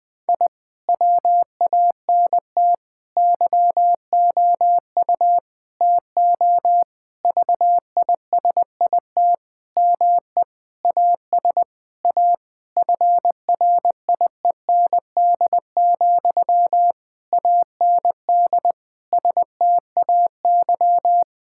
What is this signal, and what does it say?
Morse code, heard as a keyed tone, about 21 wpm, 700 Hz. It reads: I WANT YOU TO VISIT ME AS A FRIEND, AND STAY
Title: Sentences from Top 1000 Words - Review Part 6 - 20wpm